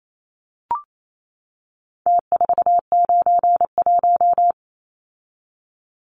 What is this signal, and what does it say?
Morse code, heard as a keyed tone, about 28 wpm, 700 Hz. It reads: T491
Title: Numbers - Contest Style - 4-Digit 28wpm